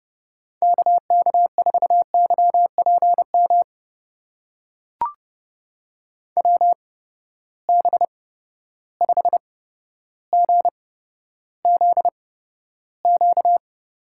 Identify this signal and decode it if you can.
Morse code, heard as a keyed tone, about 30 wpm, spaced as for 13 wpm, 700 Hz. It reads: KK4YPM WB5GZQ
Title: USA Call Signs - 6 Characters - 8x Character Spacing - 30wpm